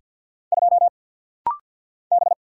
Morse code, heard as a keyed tone, about 50 wpm, 700 Hz. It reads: 2 B
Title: Single Letter-Number - Rapid-Fire 50wpm